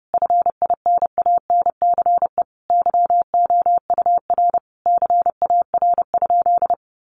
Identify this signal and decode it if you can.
Morse code, heard as a keyed tone, about 30 wpm, 700 Hz. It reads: FINANCE YOUR CAR?
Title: Sentences from Top 1500 Words 30wpm